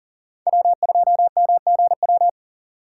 Morse code, heard as a keyed tone, about 48 wpm, 700 Hz. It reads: W2MGW
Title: Speed Racing - USA 1x3 Call Signs 40wpm